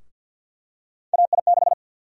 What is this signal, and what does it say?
Morse code, heard as a keyed tone, about 51 wpm, 700 Hz. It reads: AI6
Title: USA Call Sign Prefixes - Repeated 2x - 50wpm